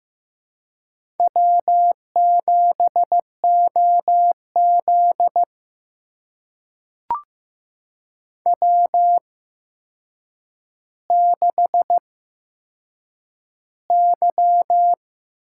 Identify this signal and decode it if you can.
Morse code, heard as a keyed tone, about 15 wpm, spaced as for 7 wpm, 700 Hz. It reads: W7OZ W6Y
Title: USA Call Signs - 4 Characters - 8x Character Spacing - 15wpm